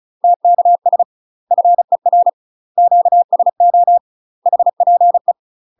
Morse code, heard as a keyed tone, about 36 wpm, 700 Hz. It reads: TKS FER QSO HPE
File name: UK MASTER QSOs TRAINING / 35WPM / UK MASTER QSOs 0028